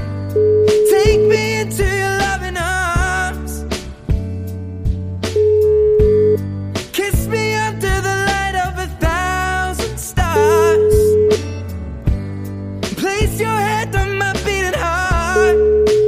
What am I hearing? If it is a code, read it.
Ringback tone (Europe, 425 Hz), with noise.